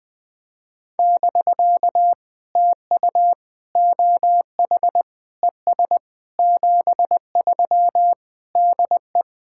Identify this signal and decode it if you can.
Morse code, heard as a keyed tone, about 20 wpm, 700 Hz. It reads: <BK> TU OH ES 73 DE
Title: POTA QSO Practice - 20wpm